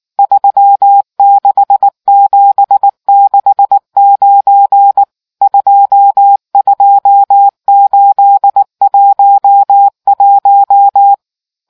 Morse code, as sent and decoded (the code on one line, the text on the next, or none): ...-- -.... --... -.... ----. / ..--- ..--- ---.. .---- .----
36769 22811